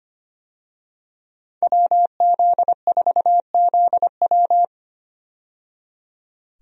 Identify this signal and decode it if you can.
Morse code, heard as a keyed tone, about 25 wpm, 700 Hz. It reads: WZ4ZW
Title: USA Call Signs - 5 Characters - 25wpm